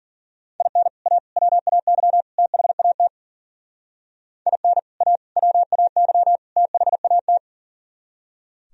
Morse code, heard as a keyed tone, about 51 wpm, 700 Hz. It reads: IN A WAY THAT IN A WAY THAT